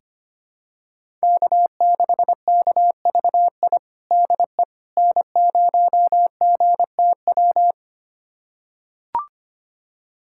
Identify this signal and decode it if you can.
Morse code, heard as a keyed tone, about 25 wpm, 700 Hz. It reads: K6KVI DE N0GTW
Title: USA Callsign DE Callsign - 5 Characters - 25wpm